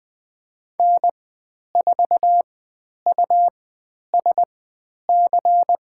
Morse code, heard as a keyed tone, about 20 wpm, spaced as for 10 wpm, 700 Hz. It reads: N4USC
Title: 102 Lesson 17g - Letter u - Call Signs 20wpm